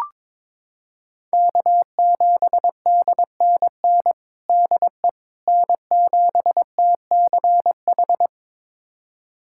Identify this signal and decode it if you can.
Morse code, heard as a keyed tone, about 22 wpm, 700 Hz. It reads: K7DNN DE N7TCH
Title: USA Callsign DE Callsign - 5 Characters - 22wpm